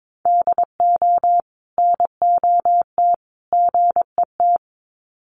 Morse code, heard as a keyed tone, about 22 wpm, 700 Hz. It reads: DO NOT GET
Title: Sets of 3 Words 22wpm